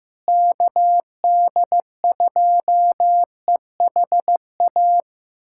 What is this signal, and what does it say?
Morse code, heard as a keyed tone, about 15 wpm, 700 Hz. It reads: KD2EHA
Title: USA Callsign DE Callsign - 6 Characters - 15wpm